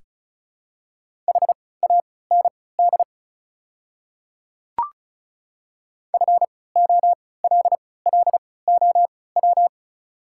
Morse code, heard as a keyed tone, about 36 wpm, spaced as for 20 wpm, 700 Hz. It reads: HAND FOLLOW
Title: 3x Character Spacing - Top 200 Words - 35wpm